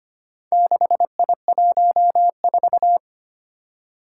Morse code, heard as a keyed tone, about 25 wpm, 700 Hz. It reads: 6I14
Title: Four Challenging Letter-Number Combinations 25wpm